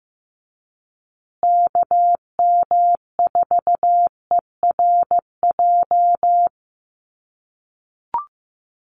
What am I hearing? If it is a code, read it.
Morse code, heard as a keyed tone, about 15 wpm, 700 Hz. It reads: KM4ERJ
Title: USA Call Signs Repeated 2x 15wpm